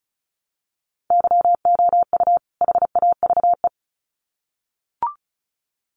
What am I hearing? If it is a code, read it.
Morse code, heard as a keyed tone, about 34 wpm, 700 Hz. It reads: YOU HAVE
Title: Sets of 2 Words 35wpm